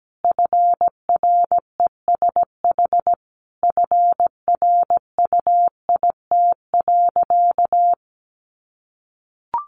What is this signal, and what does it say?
Morse code, heard as a keyed tone, about 17 wpm, 700 Hz. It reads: FRESH FRUIT.
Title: Sentences from Top 1300 Words 17wpm